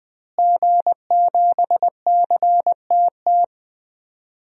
Morse code, heard as a keyed tone, about 21 wpm, 700 Hz. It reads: G7CTT
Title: DXCC-100 Call Signs - 5 Characters - Repeated 3x - 20wpm